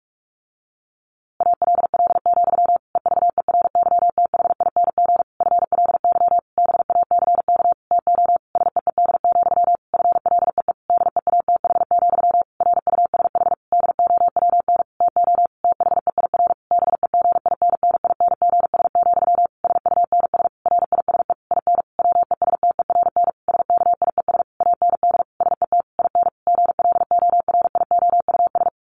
Morse code, heard as a keyed tone, about 44 wpm, 700 Hz. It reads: ALL, EVERYTHING FLY BACK TO SEED, FLEE DEATH, RUSH DOWN TO THEIR BEGINNINGS, SUNS RISE IN WESTERN SKIES AND SET IN GLORIOUS
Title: A Sound Of Thunder 45wpm